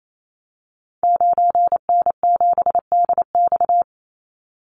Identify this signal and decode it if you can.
Morse code, heard as a keyed tone, about 28 wpm, 700 Hz. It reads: 9N7DX
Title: Call Signs Repeated 2x - 28wpm